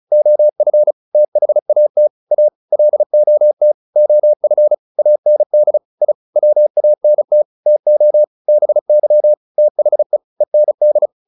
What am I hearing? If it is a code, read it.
Morse code, heard as a keyed tone, about 34 wpm, 600 Hz. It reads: OF THAT A LOT OF AND I WANT TO BY THE END